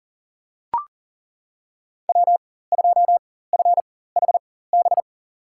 Morse code, heard as a keyed tone, about 40 wpm, spaced as for 19 wpm, 700 Hz. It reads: W2FHB
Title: USA Call Signs - 5 Characters - 4x Character Spacing - 40wpm